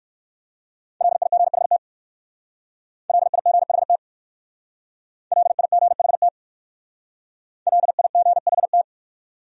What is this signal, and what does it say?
Morse code, heard as a keyed tone, about 56 wpm, 700 Hz. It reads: LIGHT LIGHT LIGHT LIGHT
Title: Speed-Racing Top 300 Words 45wpm